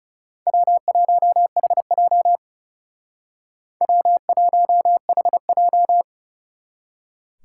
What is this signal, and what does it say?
Morse code, heard as a keyed tone, about 38 wpm, 700 Hz. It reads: W1HJ W1HJ